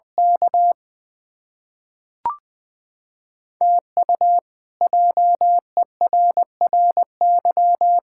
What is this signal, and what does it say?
Morse code, heard as a keyed tone, about 20 wpm, 700 Hz. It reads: K TU JERRY
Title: K1USN SST QSO Practice - 20wpm